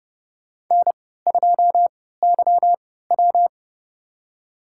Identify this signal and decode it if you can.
Morse code, heard as a keyed tone, about 32 wpm, spaced as for 17 wpm, 700 Hz. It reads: N2YW
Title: USA Call Signs - 4 Characters - 3x Character Spacing - 30wpm